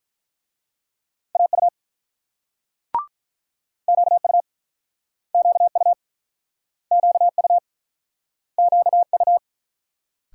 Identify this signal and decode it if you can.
Morse code, heard as a keyed tone, about 48 wpm, 700 Hz. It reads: AU QU QU QU QU